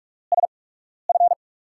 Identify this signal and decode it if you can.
Morse code, heard as a keyed tone, about 47 wpm, spaced as for 12 wpm, 700 Hz. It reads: SF
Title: USA Call Signs - 5 Characters - 8x Character Spacing - 45wpm